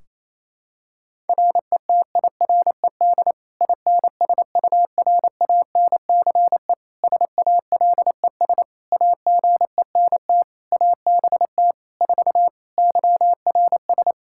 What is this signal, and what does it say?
Morse code, heard as a keyed tone, about 28 wpm, 700 Hz. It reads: RETIRED INSURANCE SALES AGENT ABT 4 YRS
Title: K1USN SST QSO Practice 28wpm